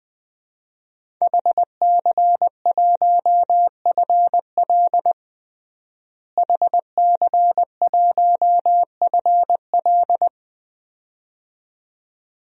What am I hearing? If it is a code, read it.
Morse code, heard as a keyed tone, about 20 wpm, 700 Hz. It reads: HC1FL HC1FL
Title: DXCC-100 Call Signs - 5 Characters - Repeated 2x - 20wpm